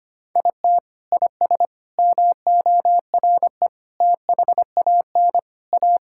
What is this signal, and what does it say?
Morse code, heard as a keyed tone, about 25 wpm, 700 Hz. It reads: IT IS MORE THAN A
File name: practice / 25wpm / Sentences with 3-Word Phrases - 25wpm